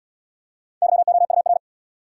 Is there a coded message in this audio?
Yes: Morse code from 0.8 to 1.6 s, heard as a keyed tone, about 69 wpm, 700 Hz. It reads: YOUR